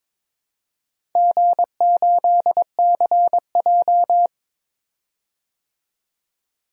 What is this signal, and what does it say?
Morse code, heard as a keyed tone, about 22 wpm, 700 Hz. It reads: G8CJ
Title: DXCC-100 Call Signs - 4 Characters - 22wpm